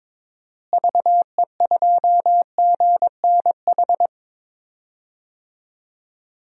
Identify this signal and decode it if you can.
Morse code, heard as a keyed tone, about 22 wpm, 700 Hz. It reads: VE2GNH
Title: DXCC-100 Call Signs - 6 Characters - Repeated 3x - 22wpm